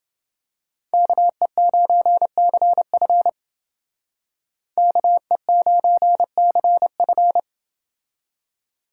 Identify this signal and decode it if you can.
Morse code, heard as a keyed tone, about 28 wpm, 700 Hz. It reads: KE9CF KE9CF